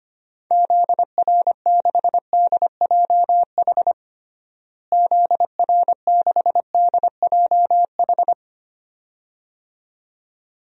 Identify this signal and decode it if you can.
Morse code, heard as a keyed tone, about 25 wpm, 700 Hz. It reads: ZR6DJH ZR6DJH
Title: DXCC-100 Call Signs - 6 Characters - Repeated 2x - 25wpm